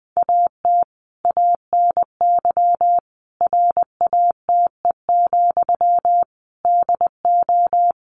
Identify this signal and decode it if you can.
Morse code, heard as a keyed tone, about 20 wpm, 700 Hz. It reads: AT ANY RATE, DO
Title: Sentences from Top 1500 Words 20wpm